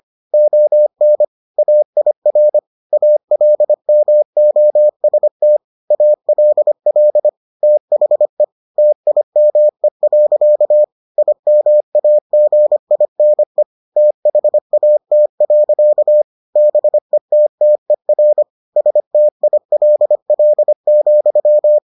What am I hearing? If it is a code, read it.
Morse code, heard as a keyed tone, about 25 wpm, 600 Hz. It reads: ON AIR ALMOST ALL THE TIME. IMAGINE THAT. BETTER STILL,